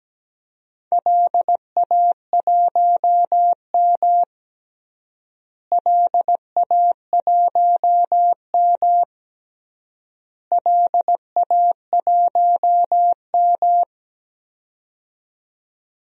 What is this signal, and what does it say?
Morse code, heard as a keyed tone, about 17 wpm, 700 Hz. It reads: LA1M LA1M LA1M
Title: DXCC-100 Call Signs - 4 Characters - Repeated 3x - 17wpm